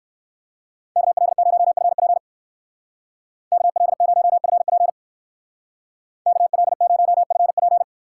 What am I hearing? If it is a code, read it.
Morse code, heard as a keyed tone, about 54 wpm, 700 Hz. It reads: KL0FP KL0FP KL0FP
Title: Speed Racing - USA 2x2 Call Signs 45wpm